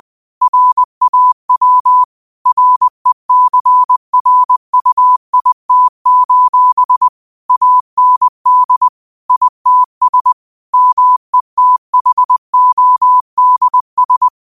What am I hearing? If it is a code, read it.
Morse code, heard as a keyed tone, about 20 wpm, 1000 Hz. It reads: RAW RECRUIT: AND ITS METHODS